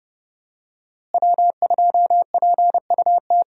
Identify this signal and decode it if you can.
Morse code, heard as a keyed tone, about 30 wpm, 700 Hz. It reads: W2PUT
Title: USA Call Signs - 5 Characters - Repeated 2x - 30wpm